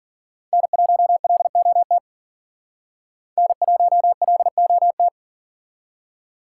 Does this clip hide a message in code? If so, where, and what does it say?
Yes: Morse code from 0.5 to 5.1 s, heard as a keyed tone, about 52 wpm, 700 Hz. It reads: N1LOT N1LOT